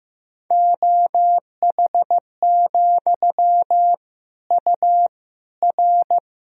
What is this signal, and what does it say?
Morse code, heard as a keyed tone, about 15 wpm, 700 Hz. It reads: OH, U R
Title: Sentences from Top 500 Words and Abbreviations 15wpm